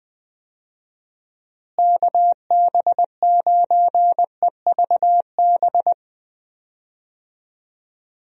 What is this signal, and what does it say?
Morse code, heard as a keyed tone, about 21 wpm, 700 Hz. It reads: KB9EVB
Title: USA Call Signs - 6 Characters - Repeated 3x - 20wpm